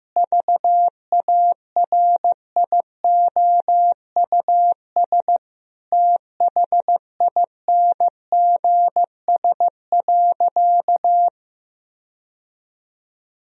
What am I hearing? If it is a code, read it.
Morse code, heard as a keyed tone, about 15 wpm, 700 Hz. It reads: VARIOUS THINGS.
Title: Sentences from Top 1000 Words - Review Part 4 - 15wpm